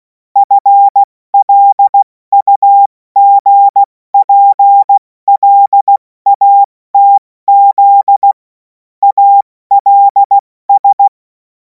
Morse code, heard as a keyed tone, about 16 wpm, spaced as for 14 wpm, 800 Hz. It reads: FLUGPLATZ ALS
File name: Morsekurs / Witze / Witz 31. März 2026 70 F80